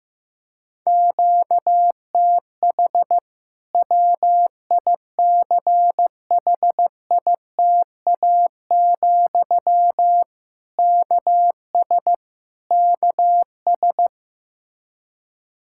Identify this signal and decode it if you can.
Morse code, heard as a keyed tone, about 15 wpm, 700 Hz. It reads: QTH WICHITA, KS KS
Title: Practice QSO - Exchange 1 - Beginning - 15wpm